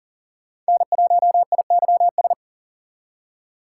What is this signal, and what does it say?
Morse code, heard as a keyed tone, about 52 wpm, 700 Hz. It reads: N1IYS